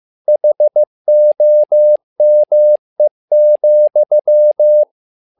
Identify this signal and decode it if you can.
Morse code, heard as a keyed tone, about 15 wpm, 600 Hz. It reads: HOME,
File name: learning center / tower trouble / tower trouble 15wpm